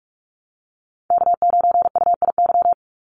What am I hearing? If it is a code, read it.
Morse code, heard as a keyed tone, about 44 wpm, 700 Hz. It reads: K9UIY
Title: Call Signs - 45wpm